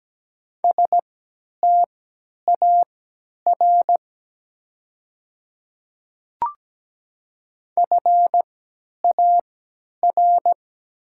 Morse code, heard as a keyed tone, about 17 wpm, spaced as for 10 wpm, 700 Hz. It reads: STAR FAR
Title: 3x Character Spacing - Top 500 Words - 17wpm